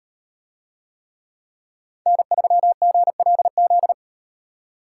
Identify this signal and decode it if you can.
Morse code, heard as a keyed tone, about 38 wpm, 700 Hz. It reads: N3GLZ